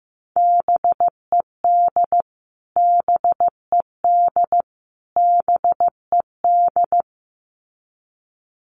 Morse code, heard as a keyed tone, about 15 wpm, 700 Hz. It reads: BED BED BED
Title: Top 500 Words Repeated 15wpm